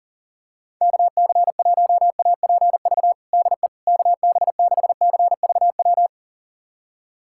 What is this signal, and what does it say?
Morse code, heard as a keyed tone, about 37 wpm, 700 Hz. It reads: KC1APV DE KB6CVW
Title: USA Callsign DE Callsign 40wpm